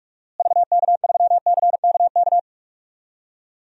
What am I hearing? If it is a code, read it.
Morse code, heard as a keyed tone, about 41 wpm, 700 Hz. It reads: VK3CKK
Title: DXCC-100 Call Signs - 6 Characters - Repeated 3x - 45wpm